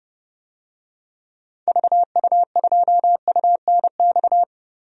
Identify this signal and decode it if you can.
Morse code, heard as a keyed tone, about 32 wpm, 700 Hz. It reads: VU2UNX
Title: DXCC-100 Call Signs - 6 Characters - Repeated 3x - 30wpm